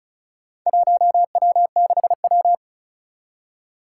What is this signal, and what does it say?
Morse code, heard as a keyed tone, about 34 wpm, 700 Hz. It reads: 1W6W
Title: Four Challenging Letter-Number Combinations 35wpm